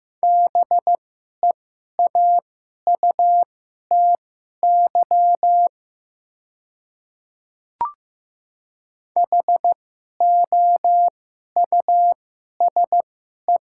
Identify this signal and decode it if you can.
Morse code, heard as a keyed tone, about 15 wpm, spaced as for 11 wpm, 700 Hz. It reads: BEAUTY HOUSE
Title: 2x Character Spacing - Top 500 Words - 15wpm